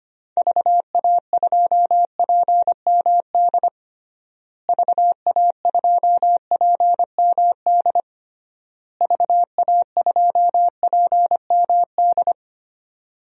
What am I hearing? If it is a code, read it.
Morse code, heard as a keyed tone, about 25 wpm, 700 Hz. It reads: VA2PMD VA2PMD VA2PMD
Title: DXCC-100 Call Signs - 6 Characters - Repeated 3x - 25wpm